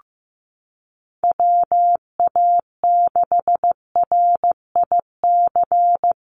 Morse code, heard as a keyed tone, about 15 wpm, 700 Hz. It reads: WA6RIC
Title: NA SOTA Chaser Callsigns 15wpm